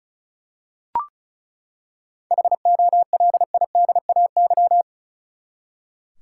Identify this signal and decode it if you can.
Morse code, heard as a keyed tone, about 34 wpm, 700 Hz. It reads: HOLIDAY
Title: Seven Letter Words - 35wpm